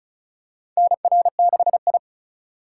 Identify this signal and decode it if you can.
Morse code, heard as a keyed tone, about 34 wpm, 700 Hz. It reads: NR6I
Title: USA Call Signs - 4 Characters - 2x Character Spacing - 35wpm